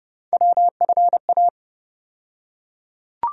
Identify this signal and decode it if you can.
Morse code, heard as a keyed tone, about 30 wpm, 700 Hz. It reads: WFA
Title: USA Call Sign Postfixes - 3 Letters 30wpm